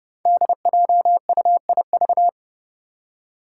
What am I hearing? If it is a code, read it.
Morse code, heard as a keyed tone, about 30 wpm, 700 Hz. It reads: DJUIV
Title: Five Challenging Letter-Number Combinations 30wpm